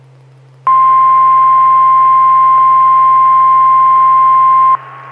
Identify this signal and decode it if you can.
Emergency Alert System (SAME) tones (the 1050 Hz attention tone). It is heard from 0.6 to 4.8 s.
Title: Saline, AK SKYWARN Repeater System - W5RHS